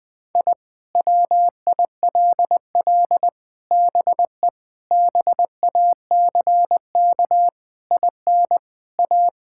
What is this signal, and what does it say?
Morse code, heard as a keyed tone, about 19 wpm, 700 Hz. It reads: I WILL BE BACK IN A
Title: Sentences with 5-Word Phrases - Spoken Beginning - 20wpm